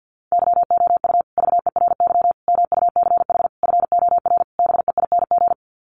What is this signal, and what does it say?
Morse code, heard as a keyed tone, about 50 wpm, 700 Hz. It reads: YOU VERY MUCH FOR BEING
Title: Sets of 5 Words Encore 50wpm